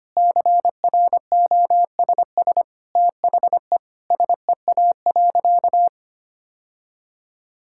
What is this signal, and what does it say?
Morse code, heard as a keyed tone, about 25 wpm, 700 Hz. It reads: CROSS THE SEA.
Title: Sentences from Top 1000 Words - Review Part 6 - 25wpm